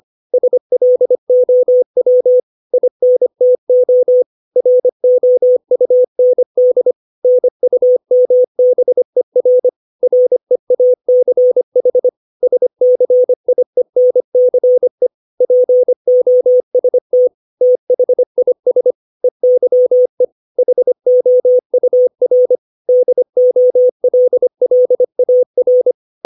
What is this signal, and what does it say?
Morse code, heard as a keyed tone, about 25 wpm, 500 Hz. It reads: SLOW INTO ROUND NUMBER REACH SCIENCE POST THIS EYE HOUR DOLLAR
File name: 632 words / 632 Words 25WPM